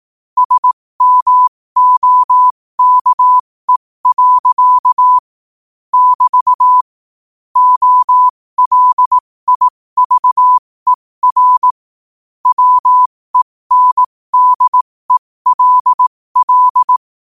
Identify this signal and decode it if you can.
Morse code, heard as a keyed tone, about 18 wpm, spaced as for 15 wpm, 1000 Hz. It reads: SMOKE. <BT> OLIVER WENDELL
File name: qotdfiles / 15wpm